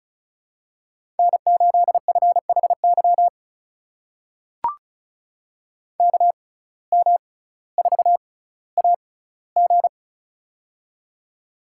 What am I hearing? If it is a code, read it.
Morse code, heard as a keyed tone, about 34 wpm, spaced as for 18 wpm, 700 Hz. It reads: N8FHY KM4AG
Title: USA Call Signs - 5 Characters - 6x Character Spacing - 35wpm